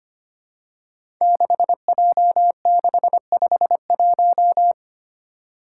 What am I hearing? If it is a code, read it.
Morse code, heard as a keyed tone, about 25 wpm, 700 Hz. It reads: 6J651
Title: Five Challenging Letter-Number Combinations 25wpm